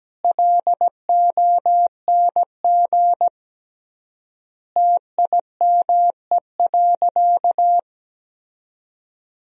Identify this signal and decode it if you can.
Morse code, heard as a keyed tone, about 17 wpm, 700 Hz. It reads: LONG TIME.
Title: 3x Word-Spacing - Sentences from Top 100 Words - 17wpm